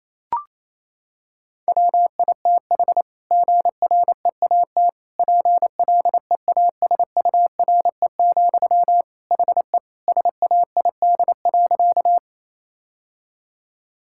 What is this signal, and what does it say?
Morse code, heard as a keyed tone, about 28 wpm, 700 Hz. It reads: WITH GREAT PLEASURE, HE SAID.
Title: Sentences from Top 1600 Words 28wpm